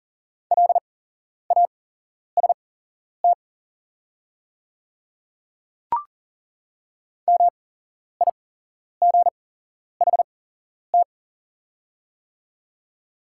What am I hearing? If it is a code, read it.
Morse code, heard as a keyed tone, about 40 wpm, spaced as for 11 wpm, 690 Hz. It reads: LAST MIGHT
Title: 8x Character Spacing - Top 300 Words - 40wpm